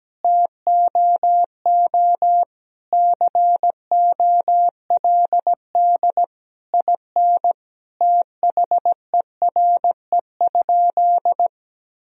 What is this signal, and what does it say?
Morse code, heard as a keyed tone, about 17 wpm, 700 Hz. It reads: TOO COLD IN THERE?
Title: Sentences from Top 1000 Words - Review Part 4 - 17wpm